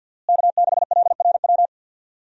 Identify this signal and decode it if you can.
Morse code, heard as a keyed tone, about 46 wpm, 700 Hz. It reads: K6LRW
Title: USA Call Signs - 5 Characters - Repeated 2x - 50wpm